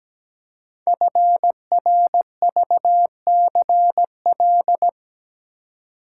Morse code, heard as a keyed tone, about 17 wpm, 700 Hz. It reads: FRVCL
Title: Code Groups - 5 Letters - 17wpm